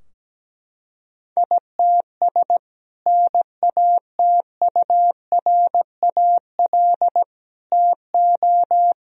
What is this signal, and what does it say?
Morse code, heard as a keyed tone, about 17 wpm, 700 Hz. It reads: ITS NATURAL TO
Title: Sentences from Top 1000 Words - Review Part 4 - 17wpm